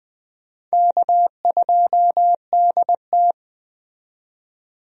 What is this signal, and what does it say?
Morse code, heard as a keyed tone, about 20 wpm, 700 Hz. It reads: K2DT
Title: USA Call Signs - 4 Characters - 20wpm